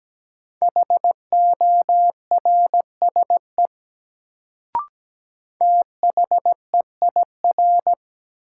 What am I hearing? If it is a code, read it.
Morse code, heard as a keyed tone, about 17 wpm, 700 Hz. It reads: HORSE THEIR
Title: Sentences from Top 300 Words - New Word Drill - 17wpm